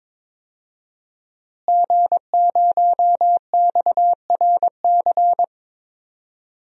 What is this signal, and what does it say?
Morse code, heard as a keyed tone, about 22 wpm, 700 Hz. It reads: G0XRC
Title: DXCC-100 Call Signs - 5 Characters - Repeated 2x - 22wpm